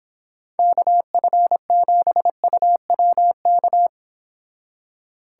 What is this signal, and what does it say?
Morse code, heard as a keyed tone, about 26 wpm, 700 Hz. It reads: KF7UWK